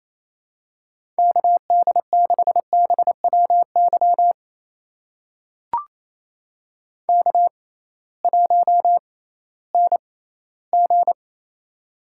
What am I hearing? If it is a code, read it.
Morse code, heard as a keyed tone, about 28 wpm, spaced as for 16 wpm, 700 Hz. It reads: KD6BWY K1NG
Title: USA Call Signs - 6x Character Spacing - 28wpm